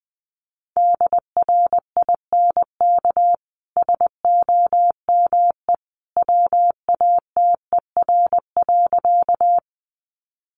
Morse code, heard as a keyed tone, about 20 wpm, 700 Hz. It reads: DRINK SOME WATER.
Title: Sentences from Top 1500 Words 20wpm